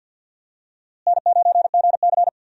Morse code, heard as a keyed tone, about 51 wpm, 700 Hz. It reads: N9GC
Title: USA Call Signs - 4 Characters - Repeated 3x - 50wpm